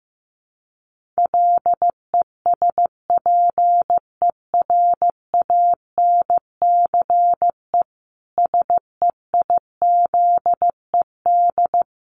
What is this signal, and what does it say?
Morse code, heard as a keyed tone, about 15 wpm, 700 Hz. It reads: LESPERANCE SEIZED
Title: A Sound Of Thunder 15wpm